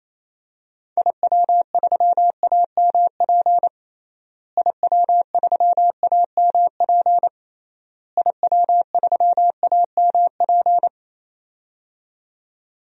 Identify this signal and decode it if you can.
Morse code, heard as a keyed tone, about 28 wpm, 700 Hz. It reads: IW3AMP IW3AMP IW3AMP
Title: DXCC-100 Call Signs - 6 Characters - Repeated 3x - 28wpm